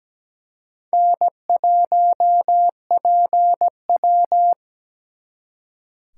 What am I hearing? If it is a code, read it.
Morse code, heard as a keyed tone, about 20 wpm, 700 Hz. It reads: N1PW
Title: Speed Racing - USA 1x2 Call Signs 17wpm